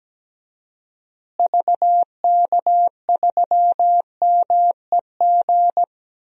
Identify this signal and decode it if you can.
Morse code, heard as a keyed tone, about 17 wpm, 700 Hz. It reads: VK3MEG
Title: DXCC-100 Call Signs - 6 Characters - Repeated 3x - 17wpm